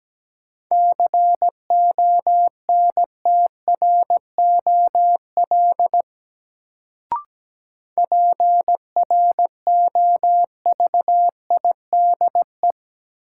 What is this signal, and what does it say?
Morse code, heard as a keyed tone, about 17 wpm, 700 Hz. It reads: CONTROL PROVIDE
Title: Sentences from Top 1000 Words - Review Part 2 - New Word Drill - 17wpm